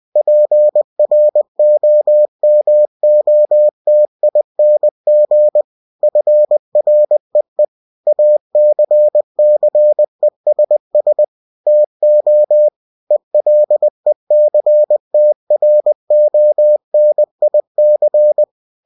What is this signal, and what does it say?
Morse code, heard as a keyed tone, about 20 wpm, 600 Hz. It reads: PROMOTING FREE ACCESS TO ELECTRONIC
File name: cw-books / winnie-the-pooh / winnie-20wpm-page-0114